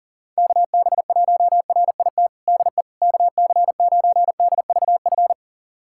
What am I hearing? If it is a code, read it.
Morse code, heard as a keyed tone, about 40 wpm, 690 Hz. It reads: KB1RIT DE KC9DVF
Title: USA Callsign DE Callsign - 6 Characters - 40wpm